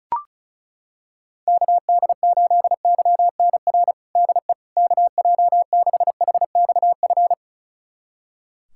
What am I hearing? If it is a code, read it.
Morse code, heard as a keyed tone, about 36 wpm, 700 Hz. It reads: KD8YNR DE KJ6HXF
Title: USA Callsign DE Callsign 35wpm